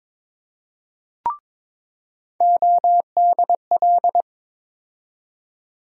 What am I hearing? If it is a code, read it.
Morse code, heard as a keyed tone, about 21 wpm, 700 Hz. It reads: ODL
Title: USA Call Sign Postfixes - 3 Letters 22wpm